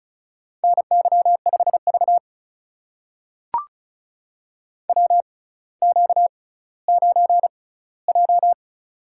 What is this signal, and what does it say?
Morse code, heard as a keyed tone, about 36 wpm, spaced as for 18 wpm, 700 Hz. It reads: NY5V WQ9J
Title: USA Call Signs - 4 Characters - 6x Character Spacing - 35wpm